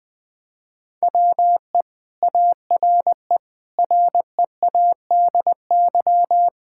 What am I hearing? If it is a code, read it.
Morse code, heard as a keyed tone, about 20 wpm, 700 Hz. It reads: WE ARE READY
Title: Sentences with 7-Word Phrases - Spoken Beginning - 20wpm